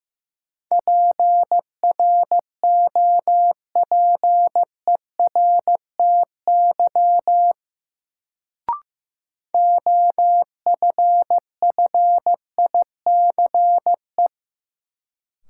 Morse code, heard as a keyed tone, about 15 wpm, 700 Hz. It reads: PROPERTY OFFICE
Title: Sentences from Top 1000 Words - New Word Drill - 15wpm